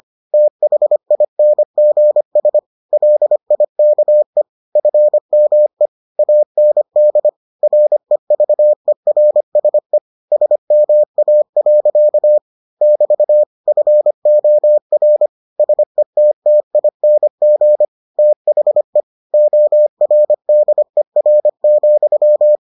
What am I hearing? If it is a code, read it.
Morse code, heard as a keyed tone, about 25 wpm, 600 Hz. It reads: THINGS LIKE FME AND REVERSE SMA. <BT> FOR SETTING THE ORDER,